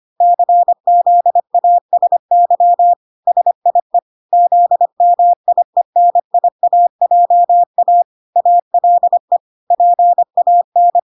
Morse code, heard as a keyed tone, about 25 wpm, 700 Hz. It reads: CZASY SIE ZMIENIAJA ALE PAN